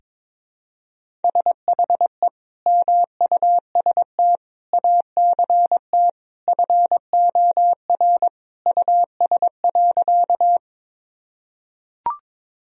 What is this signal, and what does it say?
Morse code, heard as a keyed tone, about 22 wpm, 700 Hz. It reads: SHE MUST ACT FOR US.
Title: Sentences from Top 1000 Words - Review Part 1 - 22wpm